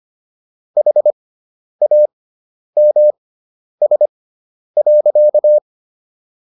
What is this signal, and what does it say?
Morse code, heard as a keyed tone, about 25 wpm, spaced as for 10 wpm, 600 Hz. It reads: HAMS.